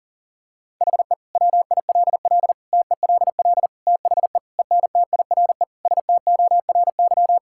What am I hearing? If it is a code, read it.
Morse code, heard as a keyed tone, about 40 wpm, 700 Hz. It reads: HE WILL TELL THE ENTIRE STORY
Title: Sentences from Top 1000 Words - Review Part 5 - 40wpm